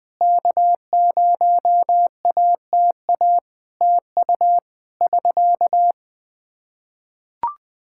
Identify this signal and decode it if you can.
Morse code, heard as a keyed tone, about 21 wpm, 700 Hz. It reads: K0ATA TU <SK>
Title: Practice QSO - Exchange 4 - Ending - 20wpm